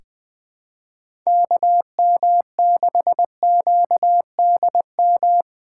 Morse code, heard as a keyed tone, about 20 wpm, 700 Hz. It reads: KM6QDM
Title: USA Call Signs - 6 Characters - 4x Character Spacing - 20wpm